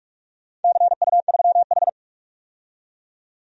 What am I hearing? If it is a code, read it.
Morse code, heard as a keyed tone, about 44 wpm, 700 Hz. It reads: CU3H
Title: DXCC-100 Call Signs - 4 Characters - 45wpm